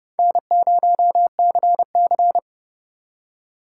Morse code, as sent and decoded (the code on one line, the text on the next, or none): -. ----- -.-. -.-.
N0CC